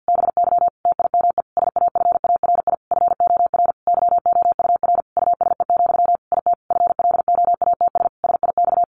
Morse code, heard as a keyed tone, about 49 wpm, 700 Hz. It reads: BY TIME SAFARI FOR YOUR USE, IT FLOATS SIX
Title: A Sound Of Thunder 50wpm